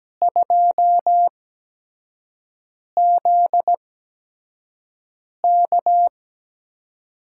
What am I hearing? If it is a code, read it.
Morse code, heard as a keyed tone, about 17 wpm, spaced as for 5 wpm, 700 Hz. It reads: 2ZK
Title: USA Call Signs - 4 Characters - 8x Character Spacing - 17wpm